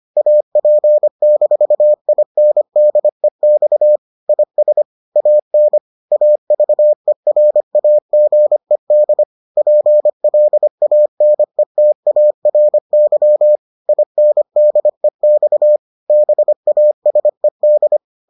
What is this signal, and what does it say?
Morse code, heard as a keyed tone, about 25 wpm, 600 Hz. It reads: AP-INDEX IS AN AVERAGED PLANETARY INDEX BASED